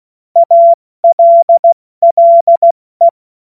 Morse code, heard as a keyed tone, about 16 wpm, 680 Hz. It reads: ALLE
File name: Witz-07-Oktober-2025-70-F80